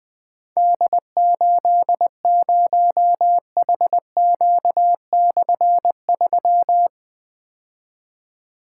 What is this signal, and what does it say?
Morse code, heard as a keyed tone, about 20 wpm, 700 Hz. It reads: D80HQ/3
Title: Difficult Call Signs 20wpm